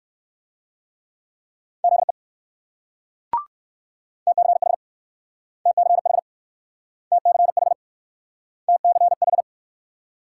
Morse code, heard as a keyed tone, about 59 wpm, 700 Hz. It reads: ZE TCH TCH TCH TCH